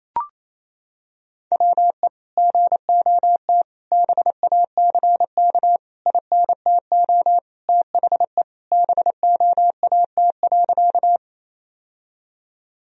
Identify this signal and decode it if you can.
Morse code, heard as a keyed tone, about 28 wpm, 700 Hz. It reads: WE GOT BACK INTO THE BOAT.
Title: Sentences from Top 1000 Words - Review Part 3 - 28wpm